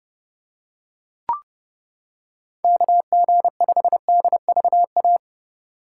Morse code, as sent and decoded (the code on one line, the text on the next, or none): -.- --. ..... -.. ...- .-
KG5DVA